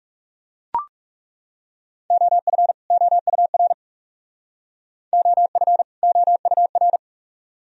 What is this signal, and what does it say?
Morse code, heard as a keyed tone, about 37 wpm, 700 Hz. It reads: OF OUR OF OUR
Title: Speed-Racing - Sets of 2 Words 30wpm